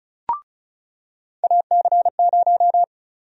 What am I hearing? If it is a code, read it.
Morse code, heard as a keyed tone, about 34 wpm, 700 Hz. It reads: AC0
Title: USA Call Sign Prefixes - 35wpm